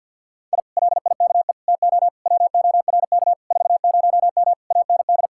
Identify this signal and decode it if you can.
Morse code, heard as a keyed tone, about 66 wpm, spaced as for 57 wpm, 700 Hz. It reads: I LIKE TO WORK 40M AND
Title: Speed-Racing Common QSO Phrases - 50wpm